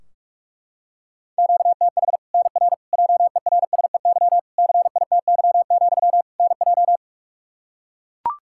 Morse code, heard as a keyed tone, about 42 wpm, 700 Hz. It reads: QTH NR JERSEY CITY, NJ
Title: K1USN SST QSO Practice 45wpm